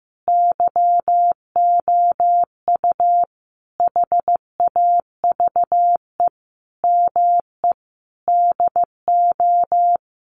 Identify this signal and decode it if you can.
Morse code, heard as a keyed tone, about 15 wpm, 700 Hz. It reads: YOU HAVE ME DO
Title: Sentences from Top 300 Words 15wpm